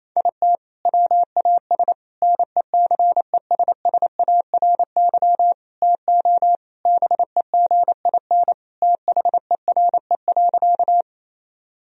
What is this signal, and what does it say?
Morse code, heard as a keyed tone, about 28 wpm, 700 Hz. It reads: IT WAS NECESSARY TO BEGIN THERE.
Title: Sentences from Top 1000 Words - Review Part 1 - 28wpm